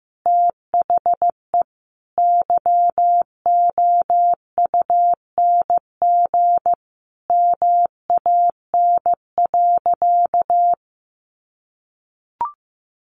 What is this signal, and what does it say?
Morse code, heard as a keyed tone, about 15 wpm, 700 Hz. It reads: THE YOUNG MAN.
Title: Sentences from Top 900 Words 15wpm